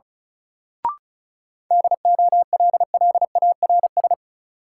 Morse code, heard as a keyed tone, about 36 wpm, 700 Hz. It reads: DOLLARS
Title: Sets of 5 Words - New Word Drill - 35wpm